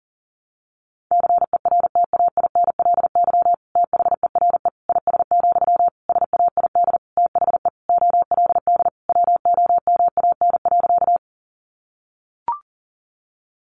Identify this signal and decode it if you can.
Morse code, heard as a keyed tone, about 40 wpm, 700 Hz. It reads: CERTAINLY THERE IS, SAID THE OLD WOMAN.
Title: Sentences from Top 1100 Words 40wpm